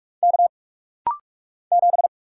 Morse code, heard as a keyed tone, about 44 wpm, 700 Hz. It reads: K 7
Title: Single Letter-Number - Rapid-Fire 45wpm